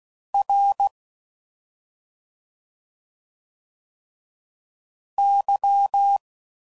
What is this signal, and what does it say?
Morse code, heard as a keyed tone, about 16 wpm, 800 Hz. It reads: R Y
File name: Suffixes / Special ING's / Sngl ING's